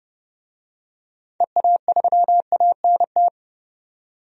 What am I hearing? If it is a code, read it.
Morse code, heard as a keyed tone, about 32 wpm, 700 Hz. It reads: EA3ANT
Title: DXCC-100 Call Signs - 6 Characters - 30wpm